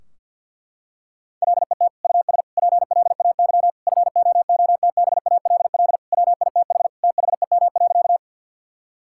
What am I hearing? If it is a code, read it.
Morse code, heard as a keyed tone, about 51 wpm, 700 Hz. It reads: LET US PLAY FOOTBALL WITH THEM.
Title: Sentences from Top 1600 Words 50wpm